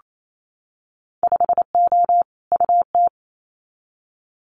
Morse code, heard as a keyed tone, about 28 wpm, 700 Hz. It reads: 5O UT
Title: Winter Field Day Exchanges 28wpm